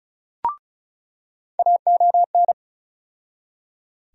Morse code, heard as a keyed tone, about 34 wpm, 700 Hz. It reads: AON
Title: USA Call Sign Postfixes - 3 Letters 35wpm